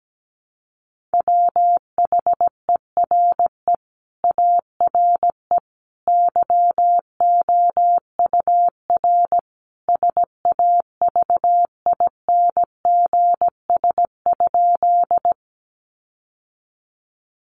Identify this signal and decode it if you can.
Morse code, heard as a keyed tone, about 17 wpm, 700 Hz. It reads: WHERE ARE YOUR SAVINGS?
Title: Sentences from Top 1200 Words 17wpm